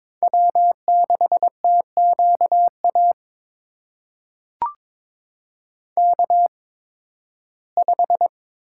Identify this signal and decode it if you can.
Morse code, heard as a keyed tone, about 22 wpm, spaced as for 14 wpm, 700 Hz. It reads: W6TQA K5
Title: USA Call Signs - 5 Characters - 8x Character Spacing - 22wpm